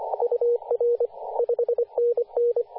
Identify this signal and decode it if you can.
Morse code, heard as a keyed tone, about 24 wpm, 480 Hz, with radio noise. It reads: UR 5NN
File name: v633ks-14-023mhz-cw